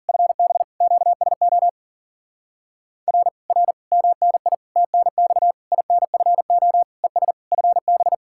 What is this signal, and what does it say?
Morse code, heard as a keyed tone, about 51 wpm, 700 Hz. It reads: FB QSO R R MNI TNX INFO ES FB